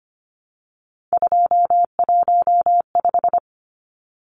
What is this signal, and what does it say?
Morse code, heard as a keyed tone, about 25 wpm, 690 Hz. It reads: 215
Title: Numbers - 3-Digit 25wpm